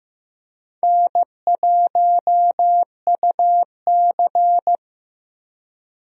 Morse code, heard as a keyed tone, about 15 wpm, 700 Hz. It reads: N1UC
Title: USA Call Signs - 4 Characters - Repeated 2x - 15wpm